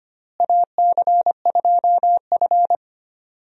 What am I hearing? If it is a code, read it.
Morse code, heard as a keyed tone, about 25 wpm, 700 Hz. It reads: AC2F
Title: USA Call Signs - 4 Characters - Repeated 3x - 25wpm